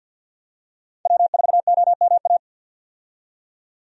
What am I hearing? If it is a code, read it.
Morse code, heard as a keyed tone, about 51 wpm, 700 Hz. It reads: W4OMA
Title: USA Call Signs - 5 Characters - 50wpm